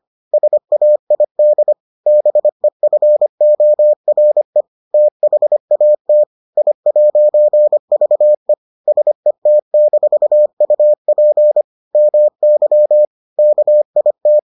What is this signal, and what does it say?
Morse code, heard as a keyed tone, about 25 wpm, 600 Hz. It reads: SAID BEFORE THAT I'VE SET-UP MY KIT